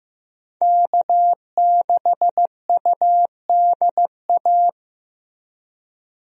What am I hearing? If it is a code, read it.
Morse code, heard as a keyed tone, about 16 wpm, 700 Hz. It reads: K6UDA
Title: USA Call Signs - 5 Characters - Repeated 2x - 15wpm